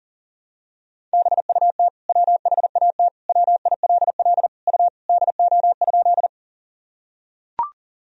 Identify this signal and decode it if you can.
Morse code, heard as a keyed tone, about 40 wpm, 700 Hz. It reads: BUT WHAT WILL U DO?
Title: Sentences from Top 100 Words and Abbreviations 40wpm